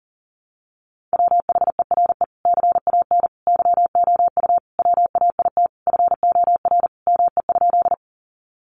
Morse code, heard as a keyed tone, about 40 wpm, 700 Hz. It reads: WHERE CAN YOU WAIT FOR ME?
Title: Sentences from Top 500 Words 40wpm